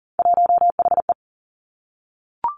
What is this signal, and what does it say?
Morse code, heard as a keyed tone, about 39 wpm, 700 Hz. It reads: 1HE
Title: Three Challenging Letter-Number Combinations 40wpm